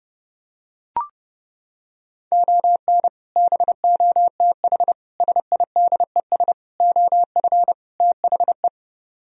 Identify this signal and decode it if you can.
Morse code, heard as a keyed tone, about 32 wpm, 700 Hz. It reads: ON BOTH SIDES OF THE